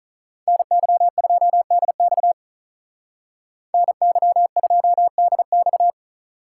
Morse code, heard as a keyed tone, about 43 wpm, 700 Hz. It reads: NY2DX NY2DX
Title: Speed Racing - USA 2x2 Call Signs 35wpm